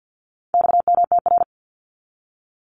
Morse code, heard as a keyed tone, about 50 wpm, 700 Hz. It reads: XMTR
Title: Abbreviations 50wpm